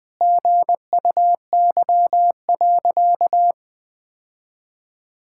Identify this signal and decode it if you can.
Morse code, heard as a keyed tone, about 21 wpm, 700 Hz. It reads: GUY.
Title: Sentences with 3-Word Phrases - 20wpm